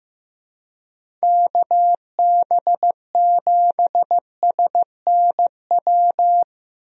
Morse code, heard as a keyed tone, about 15 wpm, 700 Hz. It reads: KB7SNW
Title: USA Call Signs - 6 Characters - Repeated 3x - 15wpm